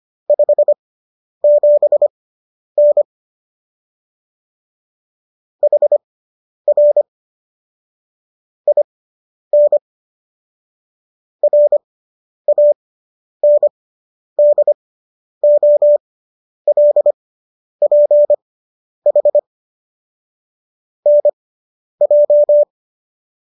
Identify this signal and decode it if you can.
Morse code, heard as a keyed tone, about 25 wpm, spaced as for 10 wpm, 590 Hz. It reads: 57N HR IN RANDOLPH NJ